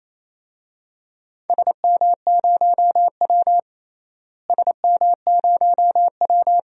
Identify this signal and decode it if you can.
Morse code, heard as a keyed tone, about 28 wpm, 700 Hz. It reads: SM0W SM0W
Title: DXCC-100 Call Signs - 4 Characters - Repeated 2x - 28wpm